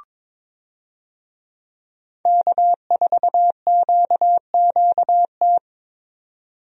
Morse code, heard as a keyed tone, about 22 wpm, 700 Hz. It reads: K4QQT
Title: USA Call Signs - 5 Characters - 22wpm